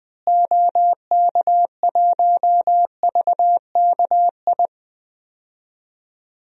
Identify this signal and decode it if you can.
Morse code, heard as a keyed tone, about 21 wpm, 700 Hz. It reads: OK1VKI
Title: DXCC-100 Call Signs - 6 Characters - Repeated 2x - 20wpm